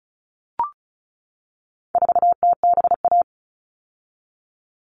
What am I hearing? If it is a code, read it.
Morse code, heard as a keyed tone, about 34 wpm, 700 Hz. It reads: 4TBA
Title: All Cut Numbers - 4-Digit 35wpm